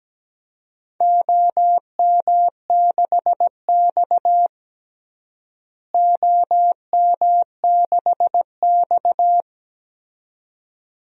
Morse code, heard as a keyed tone, about 17 wpm, 700 Hz. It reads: OM6X OM6X
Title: DXCC-100 Call Signs - 4 Characters - Repeated 2x - 17wpm